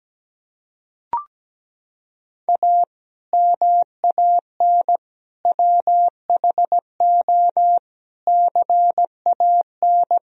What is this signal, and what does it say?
Morse code, heard as a keyed tone, about 17 wpm, 700 Hz. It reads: A MAN WHO CAN
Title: Sentences from Top 1000 Words - Review Part 2 - 17wpm